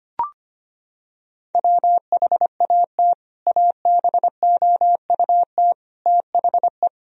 Morse code, heard as a keyed tone, about 25 wpm, 700 Hz. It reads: WHAT ABOUT THE
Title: Sentences from Top 1000 Words - Review Part 2 - 25wpm